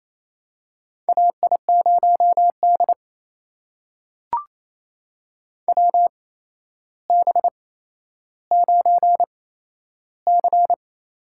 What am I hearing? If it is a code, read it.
Morse code, heard as a keyed tone, about 28 wpm, spaced as for 12 wpm, 700 Hz. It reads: AI0D WB9C
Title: USA Call Signs - 4 Characters - 8x Character Spacing - 28wpm